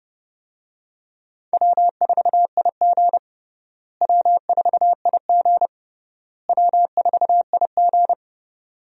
Morse code, heard as a keyed tone, about 31 wpm, 700 Hz. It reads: W4IG W4IG W4IG
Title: USA Call Signs - 4 Characters - Repeated 3x - 30wpm